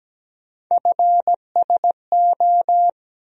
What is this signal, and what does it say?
Morse code, heard as a keyed tone, about 17 wpm, 700 Hz. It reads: FSO
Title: USA Call Sign Postfixes - 3 Letters 17wpm